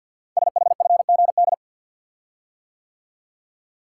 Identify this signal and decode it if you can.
Morse code, heard as a keyed tone, about 50 wpm, 700 Hz. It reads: SHFGD
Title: Code Groups - 5 Letters - 50wpm